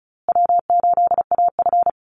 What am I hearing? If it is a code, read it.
Morse code, heard as a keyed tone, about 34 wpm, 700 Hz. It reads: W8AF
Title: USA Call Signs Repeated 2x 35wpm